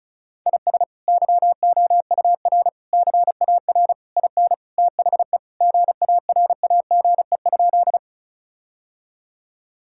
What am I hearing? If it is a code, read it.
Morse code, heard as a keyed tone, about 36 wpm, 700 Hz. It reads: IS YOUR CAR IN THE GARAGE?
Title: Sentences from Top 1600 Words 35wpm